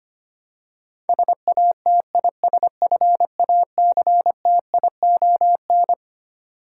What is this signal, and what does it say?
Morse code, heard as a keyed tone, about 25 wpm, 700 Hz. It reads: SATISFACTION
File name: Twelve Letter Words - 25wpm